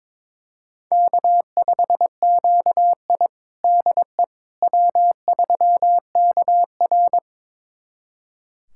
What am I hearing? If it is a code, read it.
Morse code, heard as a keyed tone, about 22 wpm, 700 Hz. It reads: K5QI DE W3KR
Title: USA Callsign DE Callsign - 4 Characters - 22wpm